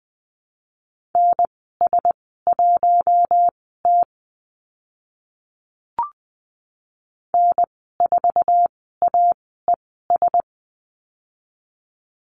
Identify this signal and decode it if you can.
Morse code, heard as a keyed tone, about 20 wpm, spaced as for 14 wpm, 700 Hz. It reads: NS1T N4AES
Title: 048 Lesson 9g - Number 4 - Call Signs 20wpm